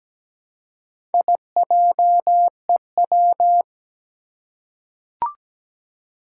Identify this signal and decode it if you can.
Morse code, heard as a keyed tone, about 17 wpm, 700 Hz. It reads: IJEW
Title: Four Challenging Letter-Number Combinations 17wpm